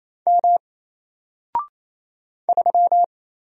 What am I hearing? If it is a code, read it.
Morse code, heard as a keyed tone, about 28 wpm, 700 Hz. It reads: M 3
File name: Single-Letter-Number-28wpm